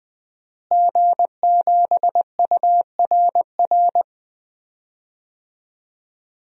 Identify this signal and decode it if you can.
Morse code, heard as a keyed tone, about 20 wpm, 700 Hz. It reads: G7URR
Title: DXCC-100 Call Signs - 5 Characters - Repeated 3x - 20wpm